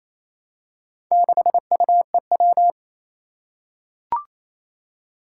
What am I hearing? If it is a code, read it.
Morse code, heard as a keyed tone, about 29 wpm, 700 Hz. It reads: 6UEW
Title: Four Challenging Letter-Number Combinations 28wpm